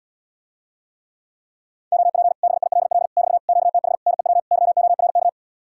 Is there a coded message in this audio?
Yes: Morse code from 1.9 to 5.3 s, heard as a keyed tone, about 77 wpm, 700 Hz. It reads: OP BILL <BT> QTH NEW YORK